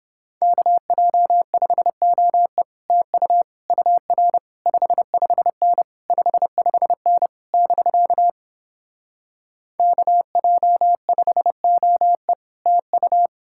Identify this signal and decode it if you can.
Morse code, heard as a keyed tone, about 29 wpm, 700 Hz. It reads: KJ5OE TU UR 55N 55N <BK> KJ5OE TU